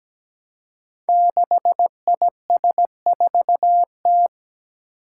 Morse code, heard as a keyed tone, about 17 wpm, 700 Hz. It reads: 6IS4T
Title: Five Challenging Letter-Number Combinations 17wpm